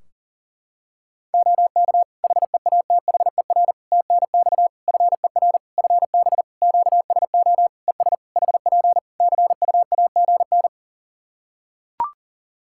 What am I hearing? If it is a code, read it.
Morse code, heard as a keyed tone, about 37 wpm, 700 Hz. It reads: OK HEATHER TNX FER FB QSO ES HP CUAGN
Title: K1USN SST QSO Practice 40wpm